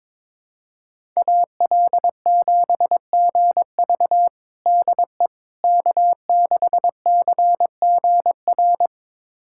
Morse code, heard as a keyed tone, about 22 wpm, 700 Hz. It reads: AL7GV DE K6CGR
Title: USA Callsign DE Callsign - 5 Characters - 22wpm